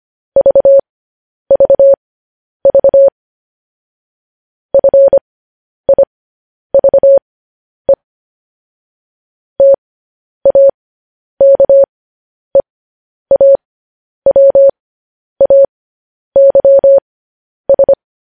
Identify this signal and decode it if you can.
Morse code, heard as a keyed tone, about 25 wpm, spaced as for 10 wpm, 550 Hz. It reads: VVV FIVE TAKEAWAYS
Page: news-headlines-in-morse-code-at-10-wpm